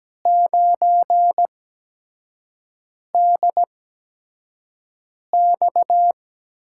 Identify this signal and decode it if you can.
Morse code, heard as a keyed tone, about 17 wpm, spaced as for 5 wpm, 700 Hz. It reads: 9DX
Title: USA Call Signs - 5 Characters - 8x Character Spacing - 17wpm